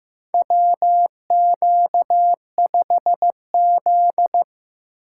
Morse code, heard as a keyed tone, about 15 wpm, 700 Hz. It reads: WQ5Z
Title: USA Call Signs - 3x Character Spacing - 15wpm